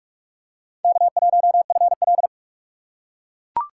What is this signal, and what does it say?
Morse code, heard as a keyed tone, about 41 wpm, 700 Hz. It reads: K1FL
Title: USA Call Signs - 4 Characters - 4x Character Spacing - 45wpm